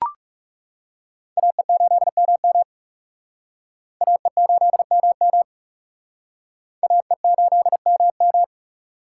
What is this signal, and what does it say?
Morse code, heard as a keyed tone, about 37 wpm, 700 Hz. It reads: AE8MM AE8MM AE8MM
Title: Speed Racing - USA 2x2 Call Signs 30wpm